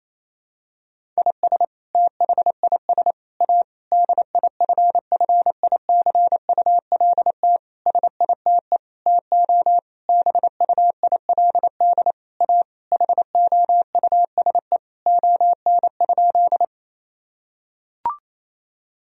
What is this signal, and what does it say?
Morse code, heard as a keyed tone, about 28 wpm, 700 Hz. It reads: IS THIS A DIFFICULT SITE TO BUILD A HOUSE ON?
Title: Sentences from Top 1000 Words - Review Part 4 - 28wpm